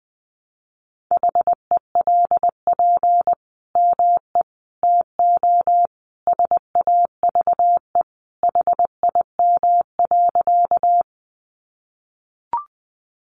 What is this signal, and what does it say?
Morse code, heard as a keyed tone, about 19 wpm, 700 Hz. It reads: HELP ME TO SAVE HIM.
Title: Sentences from Top 1100 Words 20wpm